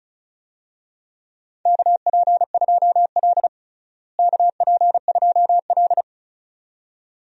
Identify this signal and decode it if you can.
Morse code, heard as a keyed tone, about 34 wpm, 700 Hz. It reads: KP2L KP2L
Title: DXCC-100 Call Signs - 4 Characters - Repeated 2x - 35wpm